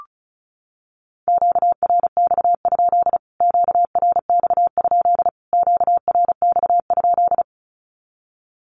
Morse code, heard as a keyed tone, about 34 wpm, 700 Hz. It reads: QRX? QRX? QRX?
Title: Q-Codes Repeated 3x 35wpm